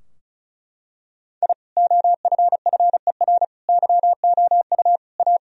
Morse code, heard as a keyed tone, about 36 wpm, 700 Hz. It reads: I OFFER YOU A
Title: Sentences from Top 1000 Words - Review Part 4 - 35wpm